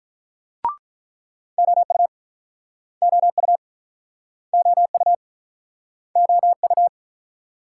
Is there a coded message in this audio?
Yes: Morse code from 1.6 to 6.9 s, heard as a keyed tone, about 49 wpm, 690 Hz. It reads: OU OU OU OU